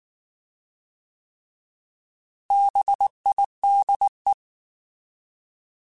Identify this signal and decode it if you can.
Morse code, heard as a keyed tone, about 19 wpm, 800 Hz. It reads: BIDE
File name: Alphabetical / Alphabetical mp3s / Bs @18